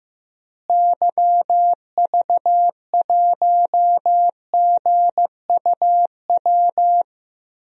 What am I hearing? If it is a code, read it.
Morse code, heard as a keyed tone, about 16 wpm, 700 Hz. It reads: YV1GUW
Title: DXCC-100 Call Signs - 6 Characters - Repeated 3x - 15wpm